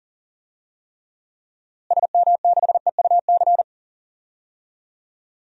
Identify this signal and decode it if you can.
Morse code, heard as a keyed tone, about 39 wpm, 700 Hz. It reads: SM6EUC
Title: DXCC-100 Call Signs - 6 Characters - Repeated 2x - 40wpm